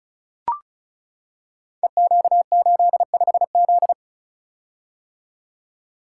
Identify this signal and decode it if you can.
Morse code, heard as a keyed tone, about 33 wpm, 700 Hz. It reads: EQ85Z
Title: Code Groups - 5 Letter:Numbers - 35wpm